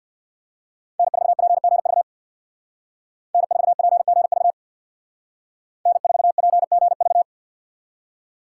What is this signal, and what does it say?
Morse code, heard as a keyed tone, about 60 wpm, 700 Hz. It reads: N4PGV N4PGV N4PGV
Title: Speed Racing - USA 1x3 Call Signs 50wpm